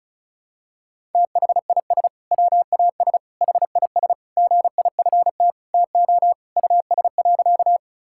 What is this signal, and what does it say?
Morse code, heard as a keyed tone, about 36 wpm, 700 Hz. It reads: THIS WAS HIS GIFT TO US.
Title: Sentences from Top 1000 Words - Review Part 6 - 35wpm